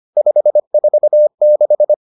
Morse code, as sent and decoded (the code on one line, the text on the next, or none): ..... ....- -....
546